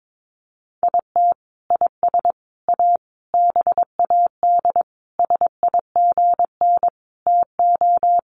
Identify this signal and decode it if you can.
Morse code, heard as a keyed tone, about 22 wpm, 700 Hz. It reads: IT IS A BAD SIGN TO
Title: Sentences from Top 1000 Words 22wpm